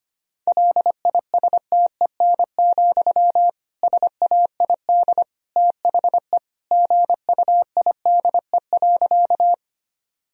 Morse code, heard as a keyed tone, about 25 wpm, 700 Hz. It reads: LISTEN, SAID THE GUIDE.
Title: Sentences from Top 1000 Words - Review Part 3 - 25wpm